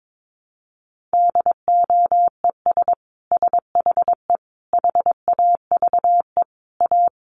Morse code, heard as a keyed tone, about 22 wpm, 700 Hz. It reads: DOES SHE HAVE A
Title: Sentences from Top 1500 Words 22wpm